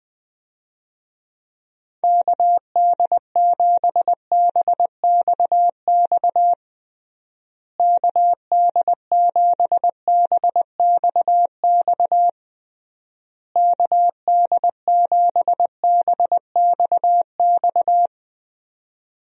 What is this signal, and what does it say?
Morse code, heard as a keyed tone, about 20 wpm, 700 Hz. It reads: KD7BXX KD7BXX KD7BXX
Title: USA Call Signs - 6 Characters - Repeated 3x - 20wpm